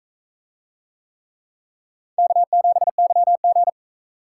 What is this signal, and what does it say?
Morse code, heard as a keyed tone, about 40 wpm, 700 Hz. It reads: K7YG